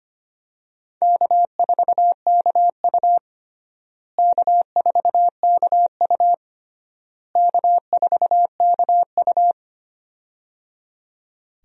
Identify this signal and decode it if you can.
Morse code, heard as a keyed tone, about 25 wpm, 700 Hz. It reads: K4KU K4KU K4KU
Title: USA Call Signs - 4 Characters - Repeated 3x - 25wpm